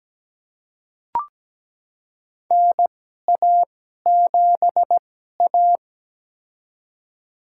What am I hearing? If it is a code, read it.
Morse code, heard as a keyed tone, about 17 wpm, spaced as for 12 wpm, 700 Hz. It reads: NA7A
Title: USA Call Signs - 4 Characters - 2x Character Spacing - 17wpm